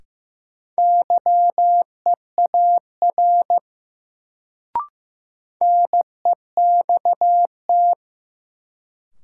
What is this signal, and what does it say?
Morse code, heard as a keyed tone, about 15 wpm, 700 Hz. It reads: YEAR NEXT
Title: Sets of 2 Words - New Word Drill - 15wpm